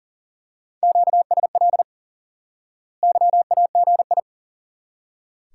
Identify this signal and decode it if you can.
Morse code, heard as a keyed tone, about 40 wpm, 700 Hz. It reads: QSL YAGI
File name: IQE Course 3 - Lesson 89 - YAGI - Two Elements - 40wpm